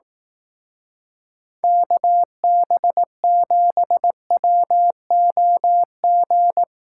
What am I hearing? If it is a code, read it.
Morse code, heard as a keyed tone, about 18 wpm, 700 Hz. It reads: KB7WOG
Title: Speed Racing - USA 2x3 Call Signs 15wpm